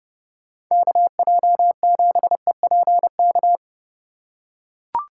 Morse code, heard as a keyed tone, about 30 wpm, 700 Hz. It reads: KJ7EPK